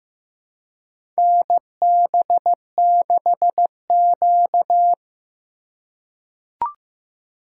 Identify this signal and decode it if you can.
Morse code, heard as a keyed tone, about 15 wpm, 700 Hz. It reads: NB6Q
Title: USA Call Signs - 4 Characters - 2x Character Spacing - 15wpm